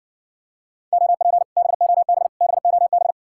Morse code, heard as a keyed tone, about 60 wpm, 700 Hz. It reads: OP BOB BOB